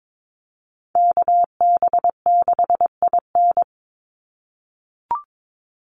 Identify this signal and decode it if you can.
Morse code, heard as a keyed tone, about 22 wpm, 700 Hz. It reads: KB6IN
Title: USA Call Signs 22wpm